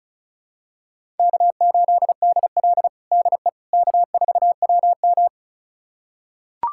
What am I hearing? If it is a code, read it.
Morse code, heard as a keyed tone, about 34 wpm, 700 Hz. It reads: K8DL DE K4WM
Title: USA Callsign DE Callsign - 4 Characters - 35wpm